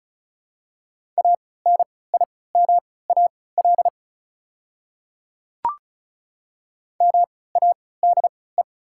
Morse code, heard as a keyed tone, about 36 wpm, spaced as for 20 wpm, 700 Hz. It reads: ANIMAL MADE
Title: 3x Character Spacing - Top 300 Words - 35wpm